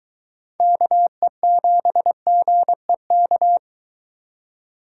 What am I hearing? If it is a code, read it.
Morse code, heard as a keyed tone, about 22 wpm, 700 Hz. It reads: KE7GEK